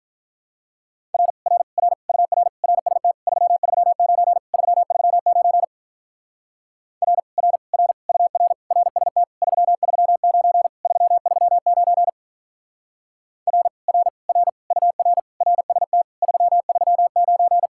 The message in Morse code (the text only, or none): R R R UR RST 339 339 R R R UR RST 339 339 R R R UR RST 339